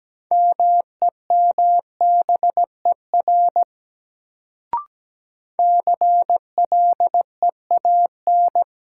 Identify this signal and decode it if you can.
Morse code, heard as a keyed tone, about 17 wpm, 700 Hz. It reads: MEMBER CLEAN
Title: Sentences from Top 1100 Words - New Word Drill Alt - 17wpm